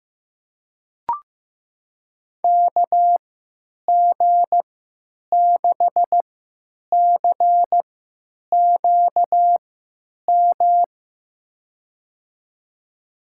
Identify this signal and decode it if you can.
Morse code, heard as a keyed tone, about 15 wpm, spaced as for 9 wpm, 700 Hz. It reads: KG6CQM
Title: USA Call Signs - 3x Character Spacing - 15wpm